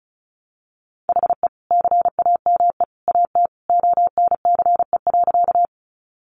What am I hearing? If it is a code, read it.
Morse code, heard as a keyed tone, about 34 wpm, 700 Hz. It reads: HE CAME AT ONCE.
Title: Sentences from Top 300 Words 35wpm